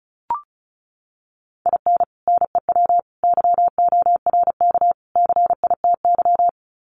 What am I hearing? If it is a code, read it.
Morse code, heard as a keyed tone, about 34 wpm, 700 Hz. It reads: IN NEW YORK CITY
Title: Sets of 4 Words 35wpm